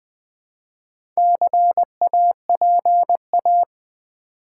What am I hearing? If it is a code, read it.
Morse code, heard as a keyed tone, about 20 wpm, 700 Hz. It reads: CAPA
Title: Build A Word - 2 Letters At A Time - 8-Letter Words - 20wpm